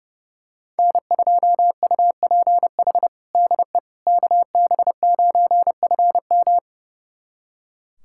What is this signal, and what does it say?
Morse code, heard as a keyed tone, about 30 wpm, 700 Hz. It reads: N2UPH DE KB9FM
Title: USA Callsign DE Callsign - 5 Characters - 30wpm